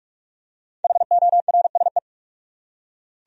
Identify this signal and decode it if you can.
Morse code, heard as a keyed tone, about 44 wpm, 700 Hz. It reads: HORSE